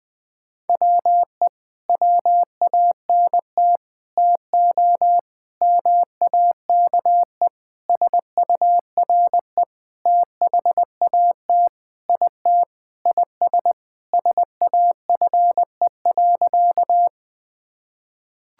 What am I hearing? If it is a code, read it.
Morse code, heard as a keyed tone, about 20 wpm, 700 Hz. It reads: WE WANT TO MAKE SURE THAT IT IS SAFE.
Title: Sentences with 6-Word Phrases - 20wpm